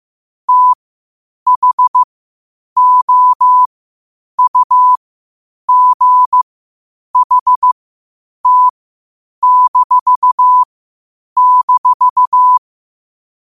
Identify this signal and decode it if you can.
Morse code, heard as a keyed tone, about 15 wpm, spaced as for 8 wpm, 1000 Hz. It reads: THOUGHT--
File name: qotdfiles / 8wpm